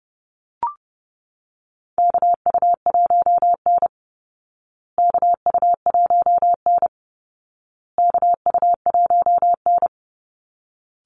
Morse code, heard as a keyed tone, about 30 wpm, 700 Hz. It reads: KU1N KU1N KU1N
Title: Call Signs Repeated 2x - 30wpm